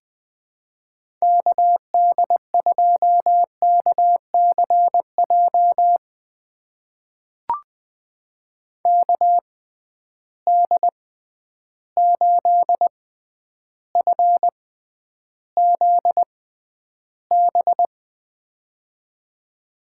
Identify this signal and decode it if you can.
Morse code, heard as a keyed tone, about 21 wpm, spaced as for 10 wpm, 700 Hz. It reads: KD2KCJ KD8FZB
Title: USA Call Signs - 6 Characters - 6x Character Spacing - 20wpm